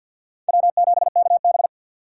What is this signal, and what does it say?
Morse code, heard as a keyed tone, about 51 wpm, 690 Hz. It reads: W7KB
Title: USA Call Signs - 4x Character Spacing - 50wpm